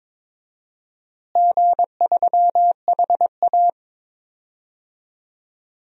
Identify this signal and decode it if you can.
Morse code, heard as a keyed tone, about 22 wpm, 700 Hz. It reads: G3HA
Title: DXCC-100 Call Signs - 4 Characters - Repeated 3x - 22wpm